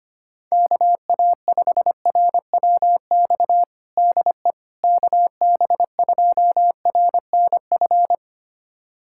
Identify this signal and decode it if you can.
Morse code, heard as a keyed tone, about 25 wpm, 700 Hz. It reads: KA5RWX DE KB2RNF
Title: USA Callsign DE Callsign - 6 Characters - 25wpm